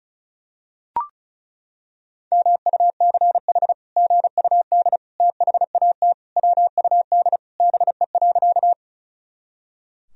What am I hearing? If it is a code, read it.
Morse code, heard as a keyed tone, about 34 wpm, 700 Hz. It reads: MUCH GUD THAT WUD BE.
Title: Sentences from Top 200 Words and Abbreviations 35wpm